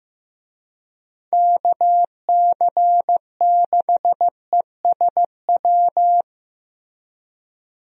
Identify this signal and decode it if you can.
Morse code, heard as a keyed tone, about 15 wpm, 700 Hz. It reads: KC6ESW
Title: USA Call Signs - 6 Characters - Repeated 2x - 15wpm